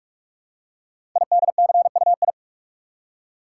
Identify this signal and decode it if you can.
Morse code, heard as a keyed tone, about 44 wpm, 700 Hz. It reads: IDCUI
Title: Code Groups - 5 Letter:Numbers - 45wpm